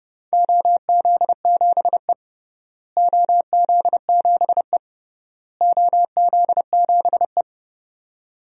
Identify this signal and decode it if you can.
Morse code, heard as a keyed tone, about 32 wpm, 700 Hz. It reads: OZ7E OZ7E OZ7E
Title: DXCC-100 Call Signs - 4 Characters - Repeated 3x - 30wpm